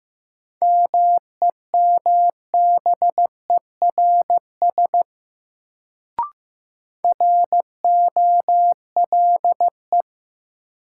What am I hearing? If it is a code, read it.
Morse code, heard as a keyed tone, about 15 wpm, 700 Hz. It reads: MEMBERS ROLE
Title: Sets of 3 Words - New Word Drill - 15wpm